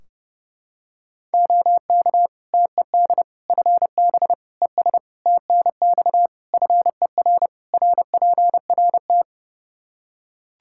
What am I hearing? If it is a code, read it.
Morse code, heard as a keyed tone, about 32 wpm, 700 Hz. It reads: OK TED FB ES TNX FER RPRT
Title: Practice QSO - Exchange 2 - Info - 30wpm